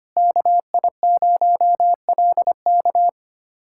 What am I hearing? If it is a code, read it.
Morse code, heard as a keyed tone, about 25 wpm, 700 Hz. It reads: KI0LK
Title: USA Callsign DE Callsign - 5 Characters - 25wpm